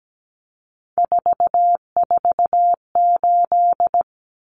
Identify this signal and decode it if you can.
Morse code, heard as a keyed tone, about 17 wpm, 700 Hz. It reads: 448
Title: Numbers - 3-Digit 17wpm